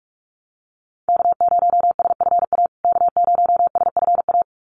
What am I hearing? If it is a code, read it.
Morse code, heard as a keyed tone, about 44 wpm, 690 Hz. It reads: K0SFA K0SFA
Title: USA Call Signs Repeated 2x 45wpm